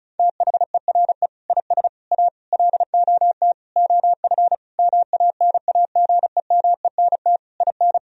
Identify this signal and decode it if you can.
Morse code, heard as a keyed tone, about 35 wpm, 700 Hz. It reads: THERE IS A LOT OF MANAGEMENT IN
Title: Sentences from Top 1000 Words - Review Part 4 - 35wpm